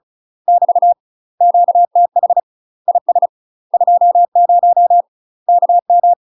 Morse code, heard as a keyed tone, about 34 wpm, 700 Hz. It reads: <BT> QTH IS 20 KM
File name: UK MASTER QSOs TRAINING / 35WPM / UK MASTER QSOs 0024